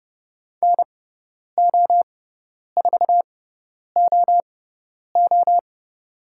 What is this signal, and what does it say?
Morse code, heard as a keyed tone, about 30 wpm, spaced as for 10 wpm, 700 Hz. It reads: NO4OO
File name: Lesson 16g - Letter c - Call Signs 30wpm